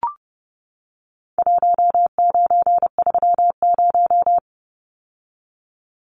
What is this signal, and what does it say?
Morse code, heard as a keyed tone, about 30 wpm, 700 Hz. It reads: 1930
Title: Numbers - Contest Style - 4-Digit 30wpm